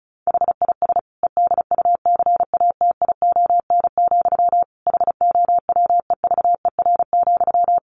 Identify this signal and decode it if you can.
Morse code, heard as a keyed tone, about 34 wpm, 700 Hz. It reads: HIS EDUCATION, HOWEVER,
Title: Sentences from Top 1400 Words 35wpm